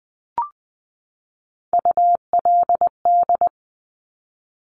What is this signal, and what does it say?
Morse code, heard as a keyed tone, about 20 wpm, 700 Hz. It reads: ULD
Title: Top Three Letter Combinations in Words 20wpm